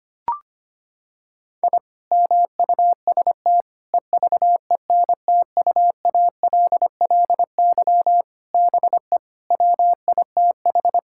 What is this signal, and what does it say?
Morse code, heard as a keyed tone, about 25 wpm, 700 Hz. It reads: I MUST EVENTUALLY BE WITH
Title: Sentences from Top 1000 Words - Review Part 6 - 25wpm